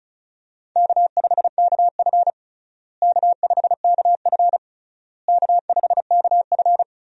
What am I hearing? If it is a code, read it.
Morse code, heard as a keyed tone, about 36 wpm, 700 Hz. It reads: K5KF K5KF K5KF
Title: USA Call Signs - 4 Characters - Repeated 3x - 35wpm